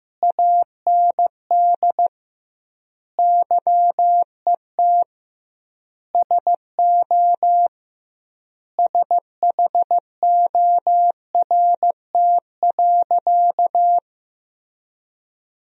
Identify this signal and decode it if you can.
Morse code, heard as a keyed tone, about 15 wpm, 700 Hz. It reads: AND YET SO SHORT.
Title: 2x Word-Spacing - Sentences from Top 500 Words - 15wpm